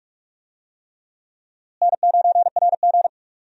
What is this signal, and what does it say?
Morse code, heard as a keyed tone, about 42 wpm, 700 Hz. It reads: N9RG
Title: USA Call Signs - 4 Characters - Repeated 3x - 45wpm